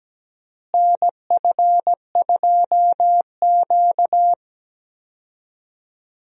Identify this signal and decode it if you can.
Morse code, heard as a keyed tone, about 17 wpm, 700 Hz. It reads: NF2Q
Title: USA Call Signs - 4 Characters - Repeated 3x - 17wpm